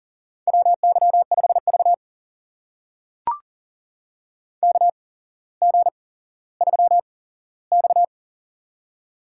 Morse code, heard as a keyed tone, about 40 wpm, spaced as for 17 wpm, 700 Hz. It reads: WY5V KG3X
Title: USA Call Signs - 4 Characters - 8x Character Spacing - 40wpm